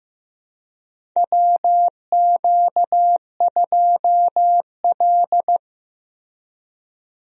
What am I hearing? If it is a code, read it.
Morse code, heard as a keyed tone, about 15 wpm, 700 Hz. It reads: WQ2L
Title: USA Call Signs - 4 Characters - Repeated 2x - 15wpm